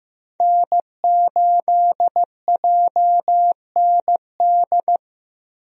Morse code, heard as a keyed tone, about 15 wpm, 700 Hz. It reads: N8JND
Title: USA Call Signs - 5 Characters - 15wpm